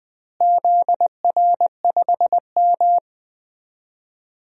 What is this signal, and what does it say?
Morse code, heard as a keyed tone, about 20 wpm, 700 Hz. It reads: ZR5M
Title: DXCC-100 Call Signs - 4 Characters - Repeated 3x - 20wpm